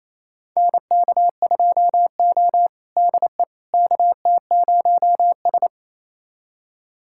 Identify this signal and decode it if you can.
Morse code, heard as a keyed tone, about 28 wpm, 700 Hz. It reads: NK2O DE KT0S
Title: USA Callsign DE Callsign - 4 Characters - 28wpm